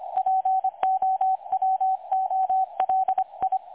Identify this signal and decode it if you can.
Morse code, heard as a keyed tone, about 25 wpm, 760 Hz, with radio noise. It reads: POWOLI
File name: dlazaawansowanych / nagraniazpasma / websdr recording 3565.0kHz